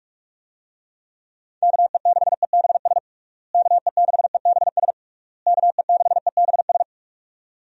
Morse code, heard as a keyed tone, about 42 wpm, 700 Hz. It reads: KE6EBS KE6EBS KE6EBS
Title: USA Call Signs - 6 Characters - Repeated 3x - 45wpm